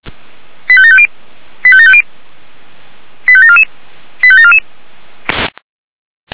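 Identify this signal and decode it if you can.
Five-tone selective call (ZVEI), heard as quick runs of five tones, tones of about 70 ms. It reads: 85750 85750 85744 85744
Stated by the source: ZVEI five-tone selective call 89262, received on SDR by feuerwehr-orlamuende (fire brigade Orlamünde).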